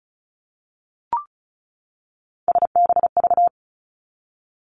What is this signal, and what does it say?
Morse code, heard as a keyed tone, about 36 wpm, 700 Hz. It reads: SBV
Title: Three Challenging Letter-Number Combinations 35wpm